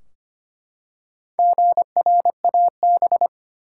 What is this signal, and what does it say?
Morse code, heard as a keyed tone, about 25 wpm, 700 Hz. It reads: GRAB
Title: Four Letter Words - 25wpm